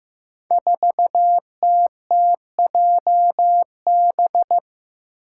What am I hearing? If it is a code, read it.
Morse code, heard as a keyed tone, about 15 wpm, 700 Hz. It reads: 4TTJB
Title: Five Challenging Letter-Number Combinations 15wpm